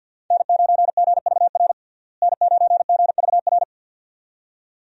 Morse code, heard as a keyed tone, about 51 wpm, 700 Hz. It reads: N9GVR N9GVR
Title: USA Call Signs - 5 Characters - Repeated 2x - 50wpm